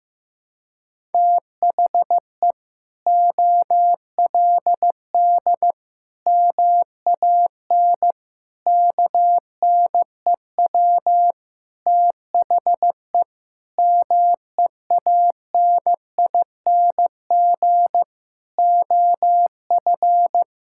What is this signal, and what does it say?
Morse code, heard as a keyed tone, about 16 wpm, 700 Hz. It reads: THE OLD MAN KNEW THE MEANING OF
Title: Sentences from Top 1000 Words - Review Part 6 - 15wpm